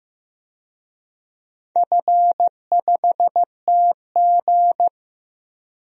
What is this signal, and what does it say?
Morse code, heard as a keyed tone, about 15 wpm, 700 Hz. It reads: F5TG
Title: DXCC-100 Call Signs - 4 Characters - Repeated 2x - 15wpm